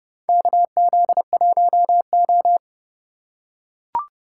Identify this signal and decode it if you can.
Morse code, heard as a keyed tone, about 32 wpm, 700 Hz. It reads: KZ1O
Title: USA Call Signs - 4 Characters - 3x Character Spacing - 30wpm